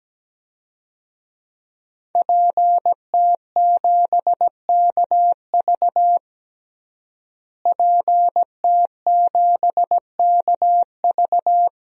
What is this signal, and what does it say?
Morse code, heard as a keyed tone, about 17 wpm, 700 Hz. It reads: PT7KV PT7KV
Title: DXCC-100 Call Signs - 5 Characters - Repeated 3x - 17wpm